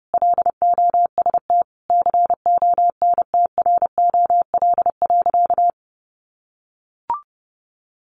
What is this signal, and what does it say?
Morse code, heard as a keyed tone, about 30 wpm, 700 Hz. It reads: LOST CONTROL.
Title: Sentences from Top 1400 Words 30wpm